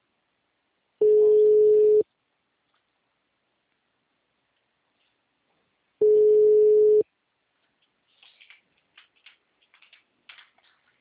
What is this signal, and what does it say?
Ringback tone (Europe, 425 Hz).
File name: out- -2017- - - .60456